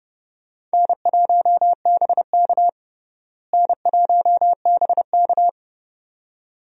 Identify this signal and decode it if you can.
Morse code, heard as a keyed tone, about 30 wpm, 700 Hz. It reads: N1BK N1BK
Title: USA Call Signs - 4 Characters - Repeated 2x - 30wpm